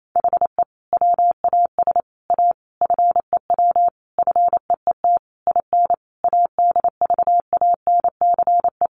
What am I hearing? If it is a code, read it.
Morse code, heard as a keyed tone, about 28 wpm, 700 Hz. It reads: HE WAS A FEW FEET IN ADVANCE
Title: Sentences from Top 1400 Words 28wpm